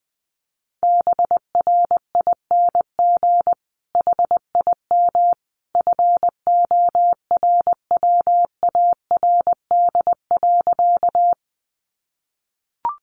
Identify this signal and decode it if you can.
Morse code, heard as a keyed tone, about 20 wpm, 700 Hz. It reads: BRING HIM FORWARD.
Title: Sentences from Top 1300 Words 20wpm